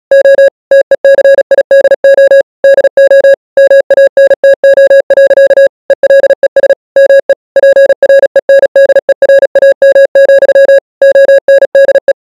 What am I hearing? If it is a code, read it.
Morse code, heard as a keyed tone, about 37 wpm, 550 Hz. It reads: O TECIDO DO MANTO. ELES ME PRENDERAM, ONDE